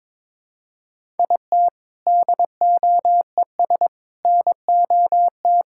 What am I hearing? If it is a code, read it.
Morse code, heard as a keyed tone, about 22 wpm, 700 Hz. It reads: IT DOES NOT
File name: practice / 22wpm / Sentences with 4-Word Phrases - 22wpm